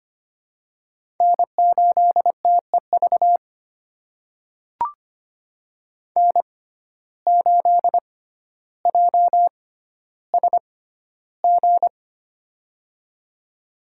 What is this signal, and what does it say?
Morse code, heard as a keyed tone, about 25 wpm, spaced as for 13 wpm, 700 Hz. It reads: N8TEV N8JSG
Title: USA Call Signs - 5 Characters - 6x Character Spacing - 25wpm